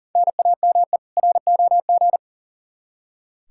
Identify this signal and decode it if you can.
Morse code, heard as a keyed tone, about 40 wpm, 700 Hz. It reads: NAME ROG
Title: Practice QSO - Exchange 1 - Beginning - 40wpm